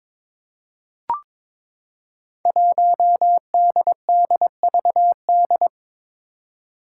Morse code, heard as a keyed tone, about 23 wpm, 700 Hz. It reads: 1DDVD
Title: Five Challenging Letter-Number Combinations 22wpm